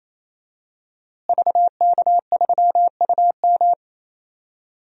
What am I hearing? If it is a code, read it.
Morse code, heard as a keyed tone, about 28 wpm, 700 Hz. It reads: VK3UM
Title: DXCC-100 Call Signs - 5 Characters - Repeated 3x - 28wpm